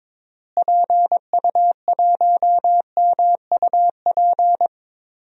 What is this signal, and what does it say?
Morse code, heard as a keyed tone, about 21 wpm, 700 Hz. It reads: PU1MUP
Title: DXCC-100 Call Signs - 6 Characters - Repeated 2x - 22wpm